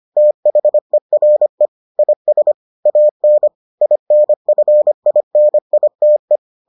Morse code, heard as a keyed tone, about 25 wpm, 600 Hz. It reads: THERE IS AN INFINITE